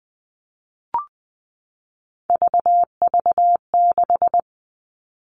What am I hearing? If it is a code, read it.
Morse code, heard as a keyed tone, about 20 wpm, 700 Hz. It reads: VV6
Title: All Cut Numbers - 3-Digit 20wpm